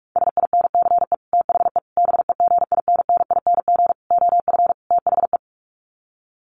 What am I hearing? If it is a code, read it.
Morse code, heard as a keyed tone, about 44 wpm, 700 Hz. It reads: SINCE THE BEGINNING OF THE
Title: Sets of 5 Words Encore2 45wpm